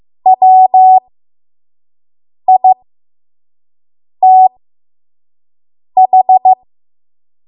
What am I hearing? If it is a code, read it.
Morse code, heard as a keyed tone, about 15 wpm, spaced as for 5 wpm, 750 Hz. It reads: WITH
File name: Morse / 05WPM